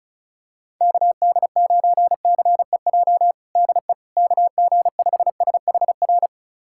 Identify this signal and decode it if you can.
Morse code, heard as a keyed tone, about 36 wpm, 690 Hz. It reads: KD9CEJ DE KG5SHR
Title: USA Callsign DE Callsign - 6 Characters - 35wpm